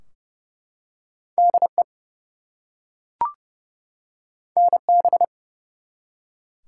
Morse code, heard as a keyed tone, about 30 wpm, 700 Hz. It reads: DE NB
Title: US States and Canadian Provinces 30wpm